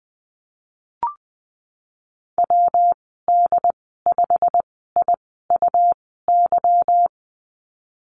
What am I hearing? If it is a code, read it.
Morse code, heard as a keyed tone, about 20 wpm, spaced as for 15 wpm, 700 Hz. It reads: WD5IUY
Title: 167 Lesson 27g - Number 7 - Call Signs 20wpm